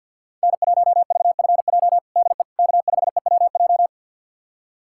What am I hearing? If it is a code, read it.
Morse code, heard as a keyed tone, about 51 wpm, 700 Hz. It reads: N1VFJ DE K5EWJ
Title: USA Callsign DE Callsign - 5 Characters - 50wpm